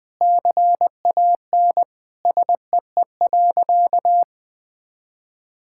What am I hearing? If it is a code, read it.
Morse code, heard as a keyed tone, about 20 wpm, 700 Hz. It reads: CAN SEE.
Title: Sentences with 4-Word Phrases - 20wpm